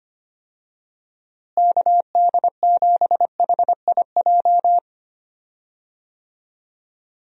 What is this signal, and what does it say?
Morse code, heard as a keyed tone, about 25 wpm, 700 Hz. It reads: KD7HIJ
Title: USA Call Signs - 6 Characters - Repeated 2x - 25wpm